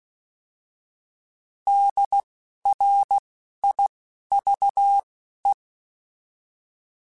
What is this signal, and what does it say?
Morse code, heard as a keyed tone, about 16 wpm, spaced as for 12 wpm, 800 Hz. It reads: DRIVE
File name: Alphabetical / Alphabetical mp3s / Ds @10